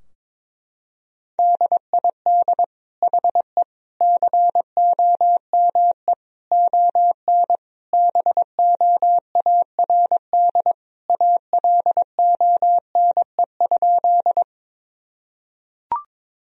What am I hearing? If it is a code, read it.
Morse code, heard as a keyed tone, about 22 wpm, 700 Hz. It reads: DID HE COME ON BOARD ALONE?
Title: Sentences from Top 1000 Words - Review Part 4 - 22wpm